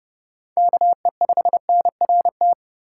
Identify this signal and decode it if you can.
Morse code, heard as a keyed tone, about 32 wpm, 700 Hz. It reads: KE5NRT
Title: USA Call Signs - 3x Character Spacing - 30wpm